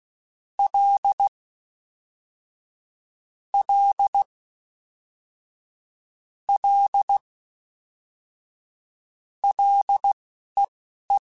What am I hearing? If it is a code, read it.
Morse code, heard as a keyed tone, about 16 wpm, spaced as for 11 wpm, 800 Hz. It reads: L L L LEE
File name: Ls / Single L m3s / Single L Prefixe 3x2 Twice @10